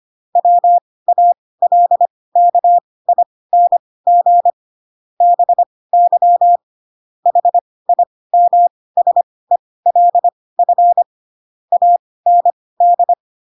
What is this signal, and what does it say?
Morse code, heard as a keyed tone, about 25 wpm, spaced as for 18 wpm, 700 Hz. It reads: WALKING BY HIMSELF AND
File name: Pinocchio ch3 12 25 ik1hge